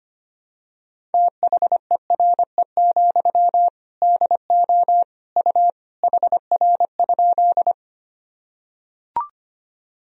Morse code, heard as a keyed tone, about 25 wpm, 700 Hz. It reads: THERE, DO U HR?
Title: Sentences from Top 300 Words and Abbreviations 25wpm